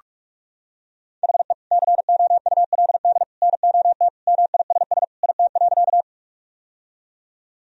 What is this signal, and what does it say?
Morse code, heard as a keyed tone, about 46 wpm, 700 Hz. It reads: HE COULD NOT MISS IT.
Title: Sentences from Top 1000 Words - Review Part 3 - 45wpm